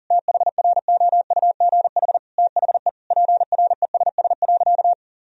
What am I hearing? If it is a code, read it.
Morse code, heard as a keyed tone, about 40 wpm, 700 Hz. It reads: THROUGH THE PRESS.
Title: Sentences from Top 1000 Words - Review Part 1 - 40wpm